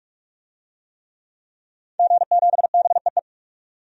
Morse code, heard as a keyed tone, about 46 wpm, 700 Hz. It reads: G7BEE
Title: DXCC-100 Call Signs - 5 Characters - 45wpm